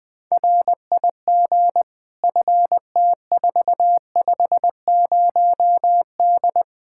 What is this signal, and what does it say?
Morse code, heard as a keyed tone, about 20 wpm, 700 Hz. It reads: RIG FT450D
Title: Practice QSO - Exchange 2 - Info - 20wpm